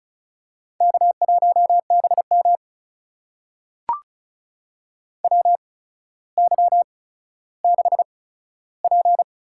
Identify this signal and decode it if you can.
Morse code, heard as a keyed tone, about 36 wpm, spaced as for 15 wpm, 700 Hz. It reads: K1BM WY6P
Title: USA Call Signs - 8x Character Spacing - 35wpm